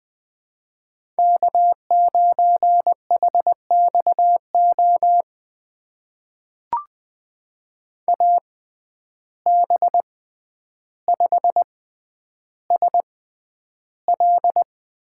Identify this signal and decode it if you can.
Morse code, heard as a keyed tone, about 20 wpm, spaced as for 10 wpm, 700 Hz. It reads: K9HXO AB5SL
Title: USA Call Signs - 5 Characters - 6x Character Spacing - 20wpm